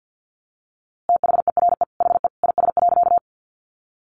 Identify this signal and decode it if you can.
Morse code, heard as a keyed tone, about 50 wpm, 700 Hz. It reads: THERE HE IS.
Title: A Sound Of Thunder 50wpm